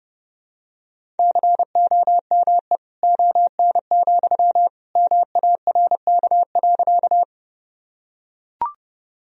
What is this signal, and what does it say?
Morse code, heard as a keyed tone, about 32 wpm, 700 Hz. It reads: COME ON, MARK.
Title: Sentences from Top 1000 Words - Review Part 1 - 30wpm